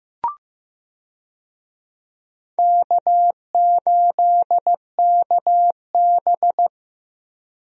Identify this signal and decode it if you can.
Morse code, heard as a keyed tone, about 15 wpm, 700 Hz. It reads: K8KB
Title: USA Call Signs - 4 Characters - Repeated 2x - 15wpm